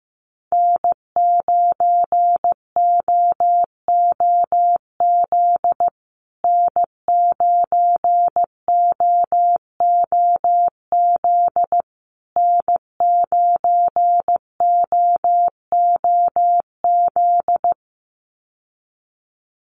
Morse code, heard as a keyed tone, about 15 wpm, 700 Hz. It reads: N9OOZ N9OOZ N9OOZ
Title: USA Call Signs Repeated 3x 15wpm